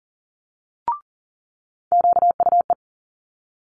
Morse code, heard as a keyed tone, about 41 wpm, 700 Hz. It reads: QUE
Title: Multi-letter Phonemes Quick Fire 40wpm